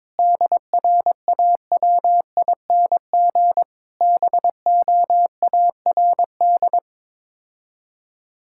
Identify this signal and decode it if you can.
Morse code, heard as a keyed tone, about 22 wpm, 700 Hz. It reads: DRAWING BOARD
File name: practice / 22wpm / Sets of 7 Words - 22wpm